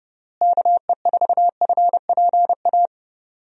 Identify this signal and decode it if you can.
Morse code, heard as a keyed tone, about 32 wpm, 700 Hz. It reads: KE4FPA
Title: USA Call Signs - 6 Characters - Repeated 3x - 30wpm